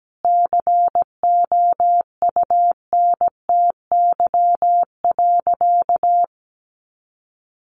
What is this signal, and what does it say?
Morse code, heard as a keyed tone, about 17 wpm, 700 Hz. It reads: COUNTY.
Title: Sentences from Top 1200 Words 17wpm